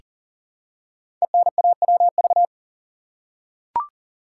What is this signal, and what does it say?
Morse code, heard as a keyed tone, about 43 wpm, 700 Hz. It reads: ENAWV
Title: Code Groups - 5 Letters - 40wpm